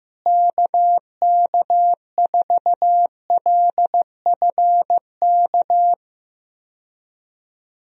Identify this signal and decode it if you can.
Morse code, heard as a keyed tone, about 15 wpm, 700 Hz. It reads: KK4LFK
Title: USA Call Signs - 6 Characters - Repeated 3x - 15wpm